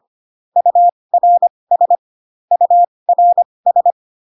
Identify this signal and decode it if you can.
Morse code, heard as a keyed tone, about 25 wpm, spaced as for 20 wpm, 700 Hz. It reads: URS URS